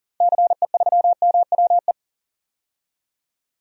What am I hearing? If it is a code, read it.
Morse code, heard as a keyed tone, about 40 wpm, 700 Hz. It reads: CE3MWE
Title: DXCC-100 Call Signs - 6 Characters - 40wpm